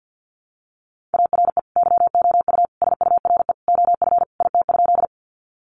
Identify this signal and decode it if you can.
Morse code, heard as a keyed tone, about 50 wpm, 700 Hz. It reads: ARE YOU SURE OF IT?
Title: Sentences from Top 300 Words 50wpm